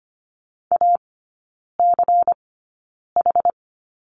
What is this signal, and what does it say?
Morse code, heard as a keyed tone, about 25 wpm, spaced as for 9 wpm, 700 Hz. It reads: ACH
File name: course / 25:10wpm / 093 Lesson 16e - Letter c - Three Characters 25wpm